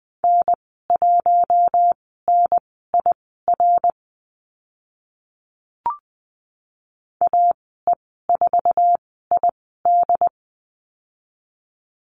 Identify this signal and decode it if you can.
Morse code, heard as a keyed tone, about 20 wpm, spaced as for 15 wpm, 700 Hz. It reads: N1NIR AE4ID
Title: 082 Lesson 14g - Number 2 - Call Signs 20wpm